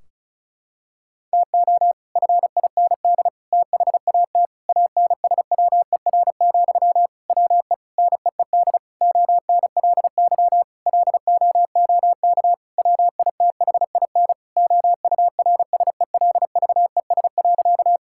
Morse code, heard as a keyed tone, about 36 wpm, 700 Hz. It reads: TO FIND THAT ANSWER, WE NEED ONLY LOOK WITHIN OURSELVES.
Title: Sentences from Top 1600 Words 35wpm